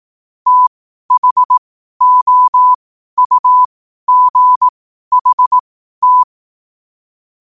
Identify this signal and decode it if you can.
Morse code, heard as a keyed tone, about 18 wpm, spaced as for 12 wpm, 1000 Hz. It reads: THOUGHT
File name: qotdfiles / 12wpm